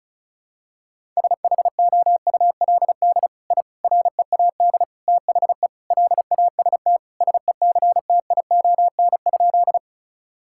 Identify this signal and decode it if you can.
Morse code, heard as a keyed tone, about 34 wpm, 700 Hz. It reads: SHOULD I READ THE LAST SECTION?
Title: Sentences from Top 1000 Words - Review Part 5 - 35wpm